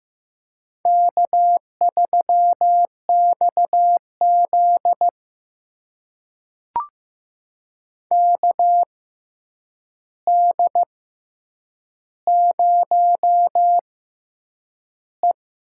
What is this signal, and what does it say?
Morse code, heard as a keyed tone, about 15 wpm, spaced as for 8 wpm, 700 Hz. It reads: K3XZ KD0E
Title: USA Call Signs - 4 Characters - 6x Character Spacing - 15wpm